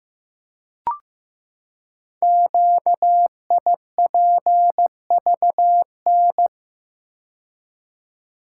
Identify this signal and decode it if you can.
Morse code, heard as a keyed tone, about 15 wpm, 700 Hz. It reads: QIPVN
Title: Code Groups - 5 Letter:Numbers - 15wpm